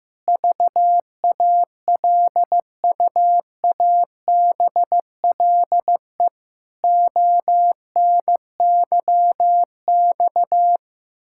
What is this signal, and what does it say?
Morse code, heard as a keyed tone, about 16 wpm, 700 Hz. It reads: VALUABLE ONYX
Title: Pangram Challenge v2 - 15wpm